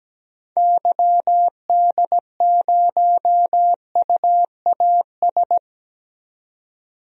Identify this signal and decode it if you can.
Morse code, heard as a keyed tone, about 17 wpm, 700 Hz. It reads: YD0UAS
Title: DXCC-100 Call Signs - 6 Characters - 17wpm